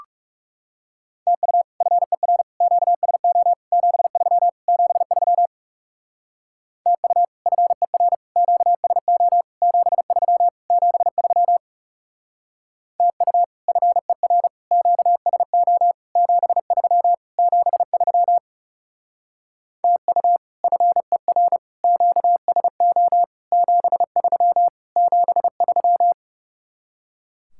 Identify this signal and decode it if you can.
Morse code, heard as a keyed tone, about 37 wpm, 700 Hz. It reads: TU FER QSO 73 73 TU FER QSO 73 73 TU FER QSO 73 73 TU FER QSO 73 73
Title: Speed-Racing Common QSO Phrases Encore - 30wpm